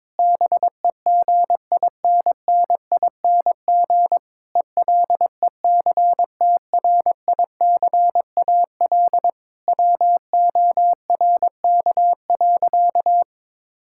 Morse code, heard as a keyed tone, about 22 wpm, 700 Hz. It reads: BEGINNING ELECTRICAL WORK.
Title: Sentences from Top 1000 Words - Review Part 6 - 22wpm